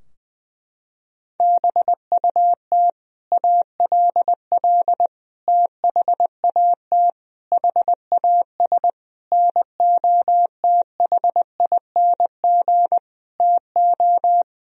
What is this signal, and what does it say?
Morse code, heard as a keyed tone, about 20 wpm, 700 Hz. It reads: BUT ALL THAT HAS NOTHING TO
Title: Sentences from Top 1000 Words - Review Part 3 - 20wpm